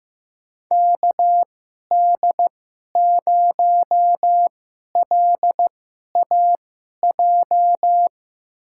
Morse code, heard as a keyed tone, about 15 wpm, spaced as for 11 wpm, 700 Hz. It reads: KD0LAJ
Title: USA Call Signs - 6 Characters - 2x Character Spacing - 15wpm